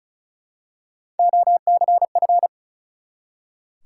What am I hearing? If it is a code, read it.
Morse code, heard as a keyed tone, about 36 wpm, 700 Hz. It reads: OCF
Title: Top 180 QSO Elements 35wpm